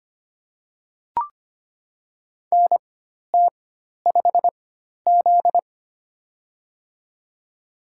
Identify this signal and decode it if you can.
Morse code, heard as a keyed tone, about 25 wpm, spaced as for 12 wpm, 700 Hz. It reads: NT5Z
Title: USA Call Signs - 4x Character Spacing - 25wpm